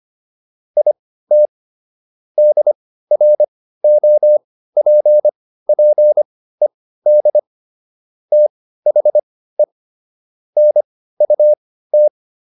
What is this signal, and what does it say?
Morse code, heard as a keyed tone, about 25 wpm, spaced as for 15 wpm, 600 Hz. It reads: IT DROPPED THE NUT
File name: SS119 15